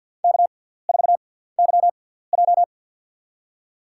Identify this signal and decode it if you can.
Morse code, heard as a keyed tone, about 50 wpm, spaced as for 17 wpm, 700 Hz. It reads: K4YJ
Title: USA Call Signs - 4 Characters - 6x Character Spacing - 50wpm